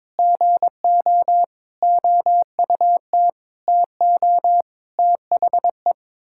Morse code, heard as a keyed tone, about 22 wpm, 700 Hz. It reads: GO OUT TO THE
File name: practice / 22wpm / Sets of 8 Words - 22wpm